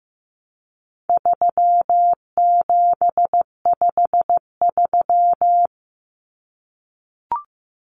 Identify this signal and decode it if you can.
Morse code, heard as a keyed tone, about 15 wpm, 700 Hz. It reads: 3753
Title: Common Cut Numbers - 4-Digit 15wpm